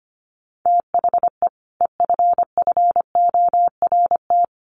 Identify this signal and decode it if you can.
Morse code, heard as a keyed tone, about 25 wpm, 700 Hz. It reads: THE EFFORT
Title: Sentences from Top 1200 Words 25wpm